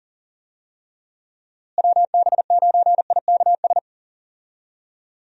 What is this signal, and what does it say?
Morse code, heard as a keyed tone, about 40 wpm, 700 Hz. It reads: WB9IKS
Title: USA Call Signs - 6 Characters - Repeated 2x - 40wpm